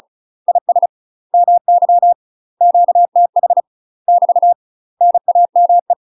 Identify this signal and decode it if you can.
Morse code, heard as a keyed tone, about 36 wpm, 700 Hz. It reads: IS MY QTH <BT> NAME
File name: UK MASTER QSOs TRAINING / 35WPM / UK MASTER QSOs 0013